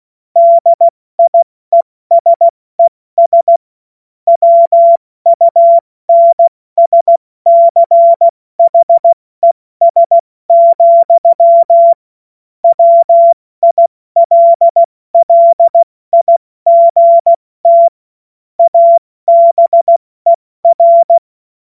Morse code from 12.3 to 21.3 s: WILLIGT ABER